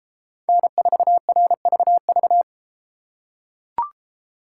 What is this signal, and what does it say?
Morse code, heard as a keyed tone, about 33 wpm, 700 Hz. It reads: N4RVV